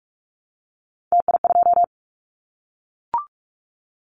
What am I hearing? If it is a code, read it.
Morse code, heard as a keyed tone, about 44 wpm, 700 Hz. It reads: TI2
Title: Callsign Prefixes for Top 100 DXCC - 45wpm